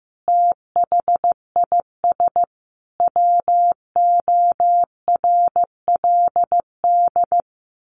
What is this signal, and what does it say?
Morse code, heard as a keyed tone, about 15 wpm, 700 Hz. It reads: THIS WORLD
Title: A Sound Of Thunder 15wpm